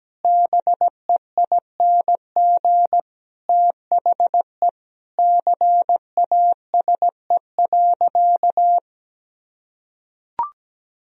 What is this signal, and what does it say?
Morse code, heard as a keyed tone, about 17 wpm, 700 Hz. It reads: BEING THE CASE.
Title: Sentences from Top 1000 Words - Review Part 3 - 17wpm